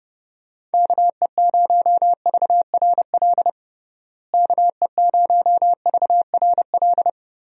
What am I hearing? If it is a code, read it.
Morse code, heard as a keyed tone, about 30 wpm, 700 Hz. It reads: KE0VRL KE0VRL
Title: USA Call Signs - 6 Characters - Repeated 3x - 30wpm